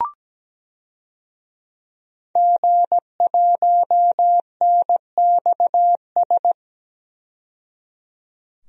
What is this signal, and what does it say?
Morse code, heard as a keyed tone, about 17 wpm, 700 Hz. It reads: G1NXS
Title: DXCC-100 Call Signs - 5 Characters - 17wpm